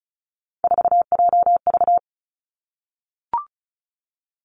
Morse code, heard as a keyed tone, about 36 wpm, 700 Hz. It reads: 4JV
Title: Three Challenging Letter-Number Combinations 35wpm